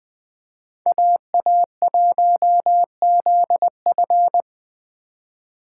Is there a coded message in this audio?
Yes: Morse code from 0.9 to 4.4 s, heard as a keyed tone, about 20 wpm, 700 Hz. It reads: AA1ZF